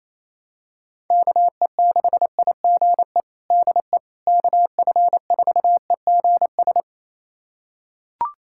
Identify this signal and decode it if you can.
Morse code, heard as a keyed tone, about 28 wpm, 700 Hz. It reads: KE6IGE DE KF4EGS
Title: USA Callsign DE Callsign - 6 Characters - 28wpm